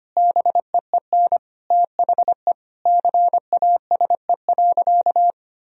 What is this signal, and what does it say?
Morse code, heard as a keyed tone, about 25 wpm, 700 Hz. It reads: BEEN THE CASE.
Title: Sentences from Top 1000 Words - Review Part 3 - 25wpm